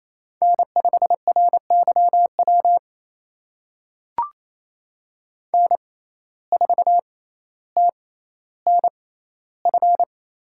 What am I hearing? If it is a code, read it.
Morse code, heard as a keyed tone, about 28 wpm, spaced as for 14 wpm, 700 Hz. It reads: N5RYW N4TNF
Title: USA Call Signs - 5 Characters - 6x Character Spacing - 28wpm